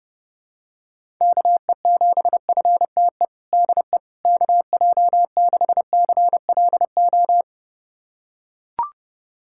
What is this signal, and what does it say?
Morse code, heard as a keyed tone, about 29 wpm, 700 Hz. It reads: KE7FTE DE KJ6CLO
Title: USA Callsign DE Callsign - 6 Characters - 30wpm